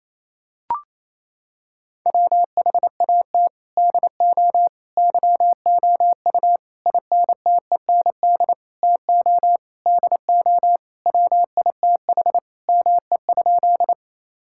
Morse code, heard as a keyed tone, about 28 wpm, 700 Hz. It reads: WHAT DO YOU INTEND TO DO WITH ME?
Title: Sentences from Top 1600 Words 28wpm